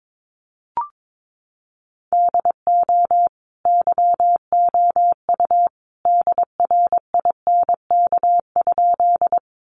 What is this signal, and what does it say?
Morse code, heard as a keyed tone, about 22 wpm, 700 Hz. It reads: DO YOU DRINK?
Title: Sentences from Top 1500 Words 22wpm